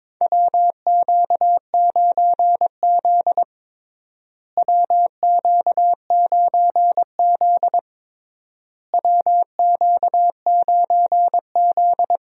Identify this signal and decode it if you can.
Morse code, heard as a keyed tone, about 22 wpm, 700 Hz. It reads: WQ9Z WQ9Z WQ9Z
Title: USA Call Signs - 4 Characters - Repeated 3x - 22wpm